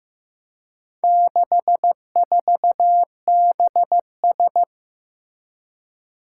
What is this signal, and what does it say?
Morse code, heard as a keyed tone, about 15 wpm, 700 Hz. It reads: 64BS
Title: Four Challenging Letter-Number Combinations 15wpm